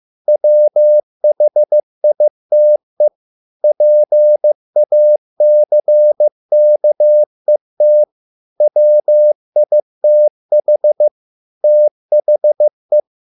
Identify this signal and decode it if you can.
Morse code, heard as a keyed tone, about 15 wpm, 600 Hz. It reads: WHITE PACKET WITH THE